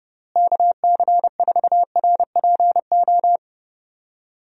Morse code, heard as a keyed tone, about 30 wpm, 700 Hz. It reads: KC4RPO
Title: USA Call Signs - 6 Characters - Repeated 3x - 30wpm